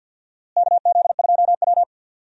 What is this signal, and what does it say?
Morse code, heard as a keyed tone, about 53 wpm, 700 Hz. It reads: KZ2W
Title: USA Call Signs - 4 Characters - Repeated 3x - 50wpm